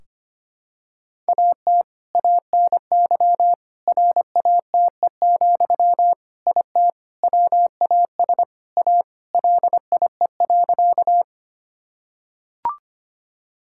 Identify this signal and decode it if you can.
Morse code, heard as a keyed tone, about 25 wpm, 700 Hz. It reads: AT ANY RATE, IT WAS A LIE.
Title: Sentences from Top 1600 Words 25wpm